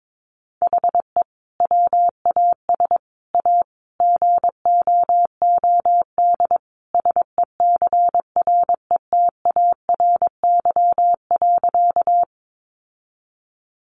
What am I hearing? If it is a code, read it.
Morse code, heard as a keyed tone, about 22 wpm, 700 Hz. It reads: HE WAS A GOOD SECRETARY.
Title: Sentences from Top 1300 Words 22wpm